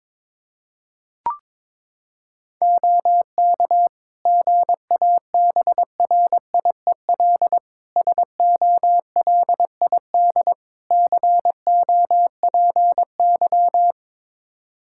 Morse code, heard as a keyed tone, about 22 wpm, 700 Hz. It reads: OK GABRIEL SOLID COPY
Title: Practice QSO - Exchange 3 - More Info - 22wpm